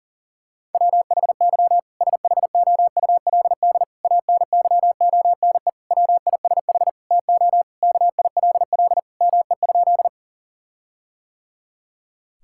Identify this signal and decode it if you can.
Morse code, heard as a keyed tone, about 40 wpm, 700 Hz. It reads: WHY SHOULD ANYONE WISH TO KILL ME?
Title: Sentences from Top 1600 Words 40wpm